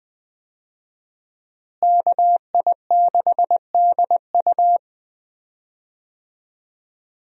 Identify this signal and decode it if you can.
Morse code, heard as a keyed tone, about 20 wpm, 700 Hz. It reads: KI6DU
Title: USA Call Signs - 5 Characters - Repeated 2x - 20wpm